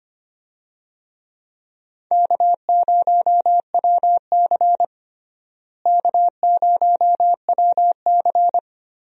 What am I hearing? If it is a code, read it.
Morse code, heard as a keyed tone, about 25 wpm, 700 Hz. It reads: K0WC K0WC
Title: USA Call Signs - 4 Characters - Repeated 3x - 25wpm